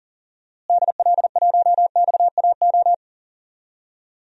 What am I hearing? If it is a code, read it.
Morse code, heard as a keyed tone, about 40 wpm, 700 Hz. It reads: DL1XAO
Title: DXCC-100 Call Signs - 6 Characters - Repeated 2x - 40wpm